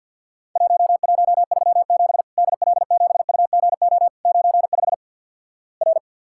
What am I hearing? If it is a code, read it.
Morse code, heard as a keyed tone, about 50 wpm, 700 Hz. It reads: 1137 DL7UGO 95 R
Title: Worked All Europe DX Contest - QTC Exchange - 50wpm